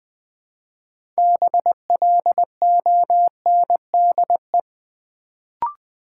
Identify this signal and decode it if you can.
Morse code, heard as a keyed tone, about 20 wpm, 700 Hz. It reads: BLONDE
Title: Build A Word - 2 Letters At A Time - 6-Letter Words - 20wpm